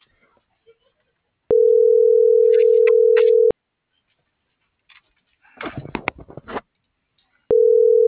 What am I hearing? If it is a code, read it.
Ringback tone (North America), 440 + 480 Hz.